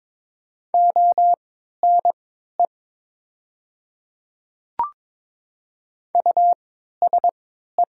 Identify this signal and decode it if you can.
Morse code, heard as a keyed tone, about 22 wpm, spaced as for 12 wpm, 700 Hz. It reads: ONE USE
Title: 3x Character Spacing - Top 100 Words - 22wpm